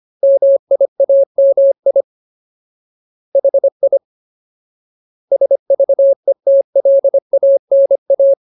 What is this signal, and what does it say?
Morse code, heard as a keyed tone, about 25 wpm, 550 Hz. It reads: MIAMI HI SVETLANA
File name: Ultimate Word List / 25WPM Ultimate